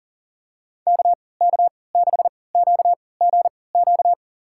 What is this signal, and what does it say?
Morse code, heard as a keyed tone, about 40 wpm, spaced as for 23 wpm, 700 Hz. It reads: KK6QGQ
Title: USA Call Signs - 6 Characters - 3x Character Spacing - 40wpm